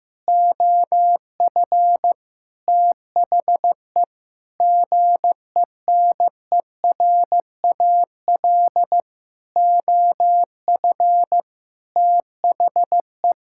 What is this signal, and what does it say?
Morse code, heard as a keyed tone, about 15 wpm, 700 Hz. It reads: OF THE GENERAL OF THE
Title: Sentences from Top 1600 Words 15wpm